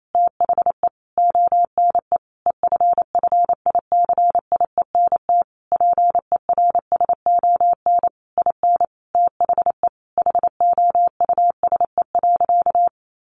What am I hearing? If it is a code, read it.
Morse code, heard as a keyed tone, about 28 wpm, 700 Hz. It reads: THE ONE EFFICIENT PERSON IN THE HOUSE.
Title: Sentences from Top 1300 Words 28wpm